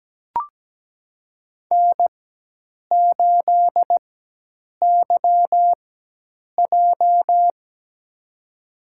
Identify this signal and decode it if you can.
Morse code, heard as a keyed tone, about 17 wpm, spaced as for 8 wpm, 700 Hz. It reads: N8YJ
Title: USA Call Signs - 4x Character Spacing - 17wpm